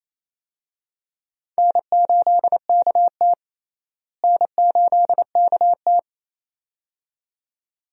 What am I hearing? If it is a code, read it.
Morse code, heard as a keyed tone, about 28 wpm, 700 Hz. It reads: N8KT N8KT
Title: USA Call Signs - 4 Characters - Repeated 2x - 28wpm